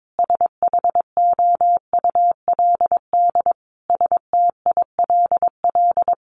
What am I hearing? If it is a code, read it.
Morse code, heard as a keyed tone, about 22 wpm, 700 Hz. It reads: SHOULD STILL
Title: Sentences from Top 1500 Words 22wpm